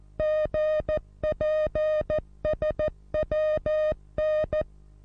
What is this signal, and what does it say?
Morse code, heard as a keyed tone, about 14 wpm, 620 Hz. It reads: GPSWN
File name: repos / Morsekurs / DJ4UF Morsen27